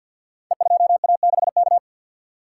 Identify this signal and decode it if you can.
Morse code, heard as a keyed tone, about 48 wpm, 700 Hz. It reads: E2A6K
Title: Code Groups - 5 Letter:Numbers - 50wpm